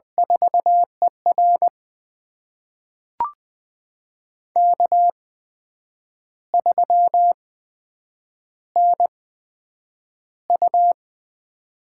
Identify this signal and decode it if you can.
Morse code, heard as a keyed tone, about 20 wpm, spaced as for 8 wpm, 700 Hz. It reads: 4ER K3NU
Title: USA Call Signs - 4 Characters - 8x Character Spacing - 20wpm